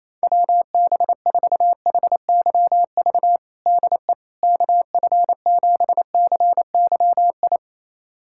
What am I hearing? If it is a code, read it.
Morse code, heard as a keyed tone, about 28 wpm, 700 Hz. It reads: WB4HYV DE KF7CYI
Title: USA Callsign DE Callsign - 6 Characters - 28wpm